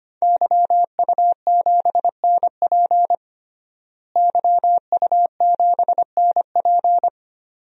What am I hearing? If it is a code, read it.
Morse code, heard as a keyed tone, about 25 wpm, 700 Hz. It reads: YU7NP YU7NP
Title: DXCC-100 Call Signs - 5 Characters - Repeated 2x - 25wpm